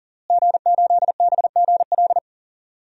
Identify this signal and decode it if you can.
Morse code, heard as a keyed tone, about 40 wpm, 700 Hz. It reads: G8BGL
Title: DXCC-100 Call Signs - 5 Characters - Repeated 3x - 40wpm